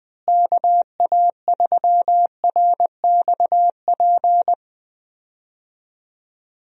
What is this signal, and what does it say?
Morse code, heard as a keyed tone, about 20 wpm, 700 Hz. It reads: KA3RXP
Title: USA Call Signs - 6 Characters - Repeated 3x - 20wpm